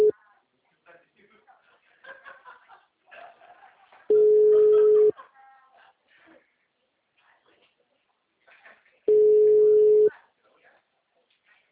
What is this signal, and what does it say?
Ringback tone (Europe, 425 Hz), with speech.
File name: out- -2040- - - .23745